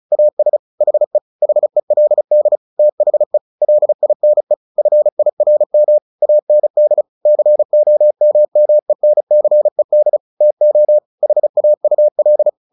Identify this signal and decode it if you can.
Morse code, heard as a keyed tone, about 34 wpm, 600 Hz. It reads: AS HE HELD THE LINE FIRM AND COMMENCED TO HAUL
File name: the old man and the sea / the old man and the sea 1 35wpm